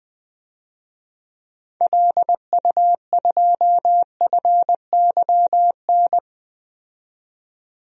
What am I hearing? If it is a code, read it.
Morse code, heard as a keyed tone, about 20 wpm, 700 Hz. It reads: LU2FYN
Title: DXCC-100 Call Signs - 6 Characters - 20wpm